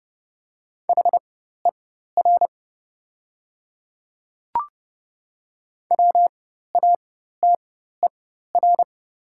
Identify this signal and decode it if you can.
Morse code, heard as a keyed tone, about 30 wpm, spaced as for 14 wpm, 700 Hz. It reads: HER WATER
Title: 4x Character Spacing - Top 100 Words - 30wpm